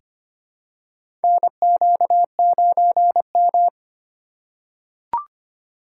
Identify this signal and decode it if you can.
Morse code, heard as a keyed tone, about 25 wpm, 700 Hz. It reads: NQ9M
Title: USA Call Signs - 4 Characters - 8x Character Spacing - 25wpm